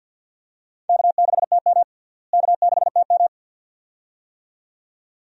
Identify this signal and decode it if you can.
Morse code, heard as a keyed tone, about 52 wpm, 700 Hz. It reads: K6TM K6TM
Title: USA Call Signs - 4 Characters - Repeated 2x - 50wpm